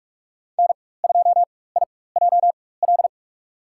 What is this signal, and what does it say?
Morse code, heard as a keyed tone, about 46 wpm, spaced as for 21 wpm, 700 Hz. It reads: N2IJL
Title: USA Call Signs - 5 Characters - 4x Character Spacing - 45wpm